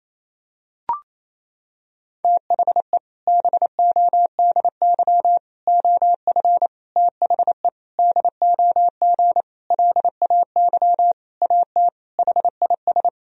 Morse code, heard as a keyed tone, about 27 wpm, 700 Hz. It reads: THE BODY OF THE DOG LAY AT HIS
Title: Sentences from Top 1000 Words - Review Part 2 - 28wpm